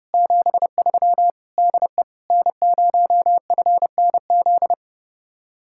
Morse code, heard as a keyed tone, about 35 wpm, 700 Hz. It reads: 73 DE N0FNZ
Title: Speed-Racing Common QSO Phrases - 30wpm